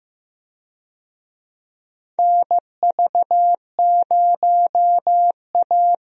Morse code, heard as a keyed tone, about 16 wpm, 700 Hz. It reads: NV0A
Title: USA Call Signs - 4 Characters - Repeated 2x - 15wpm